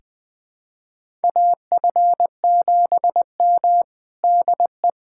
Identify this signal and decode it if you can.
Morse code, heard as a keyed tone, about 20 wpm, 700 Hz. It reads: AF7M DE
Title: USA Callsign DE Callsign - 4 Characters - 20wpm